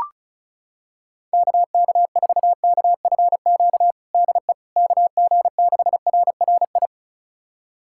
Morse code, heard as a keyed tone, about 36 wpm, 700 Hz. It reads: KK4KFQ DE KG6RRI
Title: USA Callsign DE Callsign - 6 Characters - 35wpm